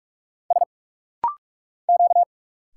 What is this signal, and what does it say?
Morse code, heard as a keyed tone, about 46 wpm, 700 Hz. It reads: S Q
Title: Single Letters - Rapid-Fire 45wpm